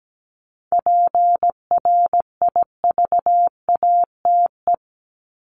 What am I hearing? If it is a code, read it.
Morse code, heard as a keyed tone, about 17 wpm, 700 Hz. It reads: PRIVATE
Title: Top 751-1000 Words 17wpm